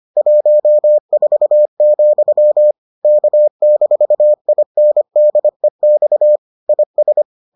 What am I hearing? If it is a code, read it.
Morse code, heard as a keyed tone, about 25 wpm, 600 Hz. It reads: 14, K-INDEX IS